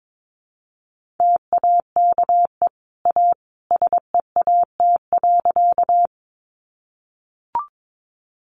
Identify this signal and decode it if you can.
Morse code, heard as a keyed tone, about 22 wpm, 700 Hz. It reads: TAKE A SEAT.
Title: Sentences from Top 1500 Words 22wpm